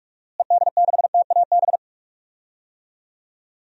Morse code, heard as a keyed tone, about 43 wpm, 700 Hz. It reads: ED6TAB
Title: DXCC-100 Call Signs - 6 Characters - Repeated 3x - 45wpm